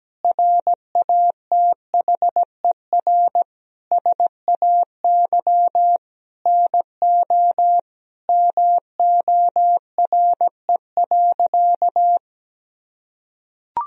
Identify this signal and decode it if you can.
Morse code, heard as a keyed tone, about 17 wpm, 700 Hz. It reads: RATHER SAY NO MORE.
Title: Sentences from Top 1000 Words - Review Part 4 - 17wpm